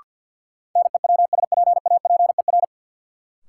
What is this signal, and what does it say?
Morse code, heard as a keyed tone, about 52 wpm, 700 Hz. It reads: NEWSPAPER
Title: Sentences from Top 1100 Words - New Word Drill Alt - 50wpm